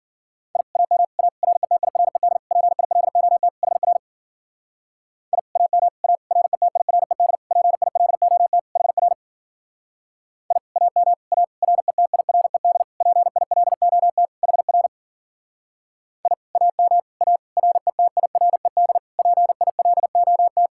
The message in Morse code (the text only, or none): I AM A RETIRED PILOT HR I AM A RETIRED PILOT HR I AM A RETIRED PILOT HR I AM A RETIRED PILOT